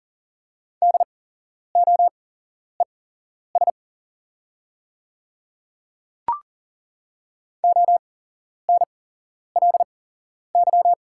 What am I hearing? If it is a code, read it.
Morse code, heard as a keyed tone, about 40 wpm, spaced as for 11 wpm, 690 Hz. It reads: DOES ONLY
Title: 8x Character Spacing - Top 300 Words - 40wpm